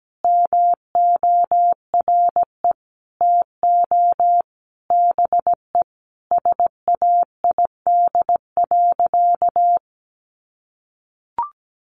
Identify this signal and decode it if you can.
Morse code, heard as a keyed tone, about 17 wpm, 700 Hz. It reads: MORE TO BE SAID.
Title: Sentences from Top 1000 Words 17wpm